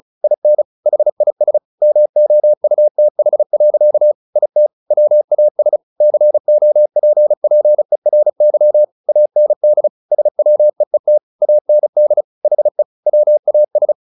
Morse code, heard as a keyed tone, about 34 wpm, 600 Hz. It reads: IN HIS MOUTH. IT WAS COPPERY AND SWEET AND HE WAS
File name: the old man and the sea 2 35wpm